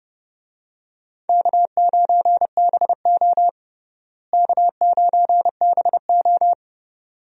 Morse code, heard as a keyed tone, about 30 wpm, 700 Hz. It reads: K9BO K9BO
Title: USA Call Signs - 4 Characters - Repeated 2x - 30wpm